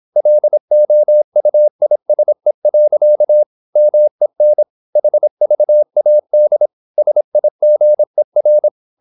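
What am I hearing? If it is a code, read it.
Morse code, heard as a keyed tone, about 26 wpm, 600 Hz. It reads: LOUISE. MEN HVAD SIGER
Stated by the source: High-speed (QRQ) Morse at 26 wpm.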